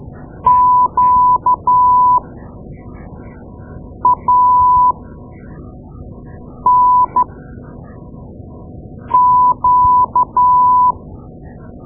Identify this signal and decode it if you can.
Morse code, heard as a keyed tone, about 12 wpm, spaced as for 4 wpm, 1000 Hz. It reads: QANQ
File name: Morse Lessons / Morse Lessons / Tape-2A